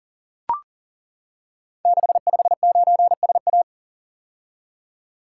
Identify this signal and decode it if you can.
Morse code, heard as a keyed tone, about 40 wpm, 690 Hz. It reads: 659SA
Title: Code Groups - 5 Letter:Numbers - 40wpm